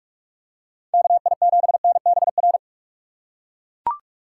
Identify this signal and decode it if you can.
Morse code, heard as a keyed tone, about 41 wpm, 700 Hz. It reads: KI7NBR
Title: USA Call Signs - 6 Characters - 3x Character Spacing - 45wpm